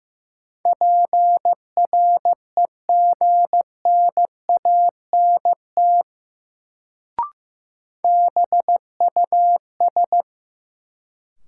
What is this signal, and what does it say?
Morse code, heard as a keyed tone, about 15 wpm, 700 Hz. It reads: PREGNANT BUS
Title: Sentences from Top 1000 Words - Review Part 6 - New Word Drill - 15wpm